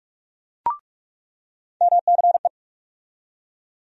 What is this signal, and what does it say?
Morse code, heard as a keyed tone, about 42 wpm, 700 Hz. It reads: MCE
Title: USA Call Sign Postfixes - 3 Letters 45wpm